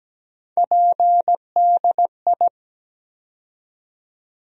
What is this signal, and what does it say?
Morse code, heard as a keyed tone, about 17 wpm, 700 Hz. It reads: PDI
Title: USA Call Sign Postfixes - 3 Letters 17wpm